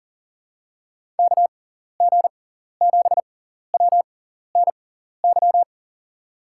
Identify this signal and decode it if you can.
Morse code, heard as a keyed tone, about 38 wpm, spaced as for 14 wpm, 700 Hz. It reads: KG7WNY
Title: USA Call Signs - 6x Character Spacing - 40wpm